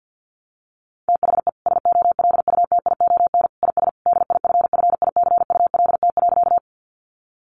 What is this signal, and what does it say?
Morse code, heard as a keyed tone, about 50 wpm, 700 Hz. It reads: THE SOLUTION IS DIFFICULT.
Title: Sentences from Top 1300 Words 50wpm